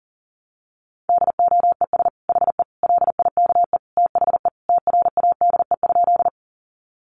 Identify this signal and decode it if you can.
Morse code, heard as a keyed tone, about 40 wpm, 700 Hz. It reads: DOES HE LIKE THE TRADE?
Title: Sentences from Top 600 Words 40wpm